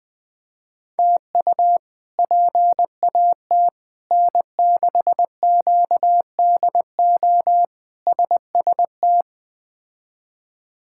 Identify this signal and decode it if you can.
Morse code, heard as a keyed tone, about 20 wpm, 700 Hz. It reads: TU PAT N6QDO SST
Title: K1USN SST QSO Practice - 20wpm